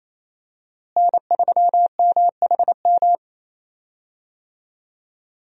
Morse code, heard as a keyed tone, about 28 wpm, 700 Hz. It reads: N3MHM
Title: USA Call Signs - 5 Characters - Repeated 3x - 28wpm